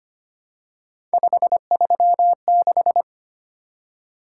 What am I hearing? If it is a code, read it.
Morse code, heard as a keyed tone, about 25 wpm, 700 Hz. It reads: 536
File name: Three Letter-Number Combinations - 25wpm